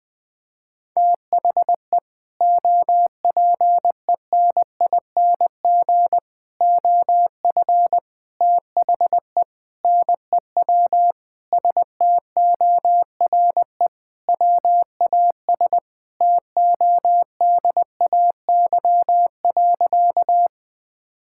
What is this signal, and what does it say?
Morse code, heard as a keyed tone, about 20 wpm, 700 Hz. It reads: THE OPENING OF THE NEW STORE WAS TODAY.
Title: Sentences with 3-Word Phrases - 20wpm